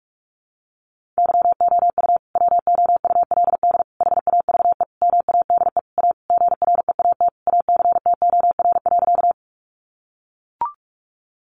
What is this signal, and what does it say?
Morse code, heard as a keyed tone, about 44 wpm, 700 Hz. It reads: YOU WOULD HAVE MADE A GREAT ACTOR.
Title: Sentences from Top 1500 Words 45wpm